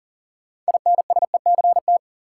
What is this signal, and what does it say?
Morse code, heard as a keyed tone, about 40 wpm, 700 Hz. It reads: INSECT
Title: Six Letter Words - 40wpm